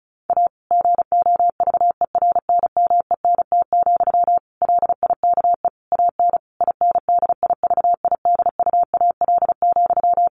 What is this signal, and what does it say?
Morse code, heard as a keyed tone, about 34 wpm, 700 Hz. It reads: A GOVERNMENT, LIKE AN INDIVIDUAL,
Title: Sentences from Top 1400 Words 35wpm